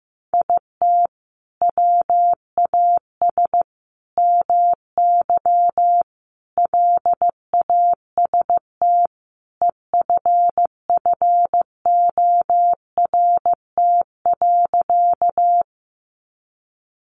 Morse code, heard as a keyed tone, about 14 wpm, 700 Hz. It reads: IT WAS MY LAST EFFORT.
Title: Sentences from Top 1200 Words 15wpm